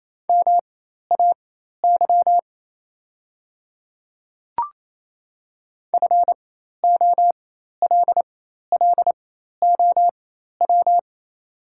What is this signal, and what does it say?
Morse code, heard as a keyed tone, about 27 wpm, spaced as for 13 wpm, 690 Hz. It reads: MAY FOLLOW
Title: 4x Character Spacing - Top 300 Words - 28wpm